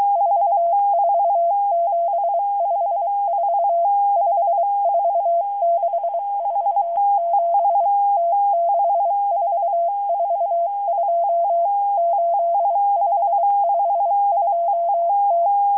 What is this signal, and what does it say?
Morse code, heard as a keyed tone, about 23 wpm, 680 Hz. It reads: 44754 54647 T6442 8552T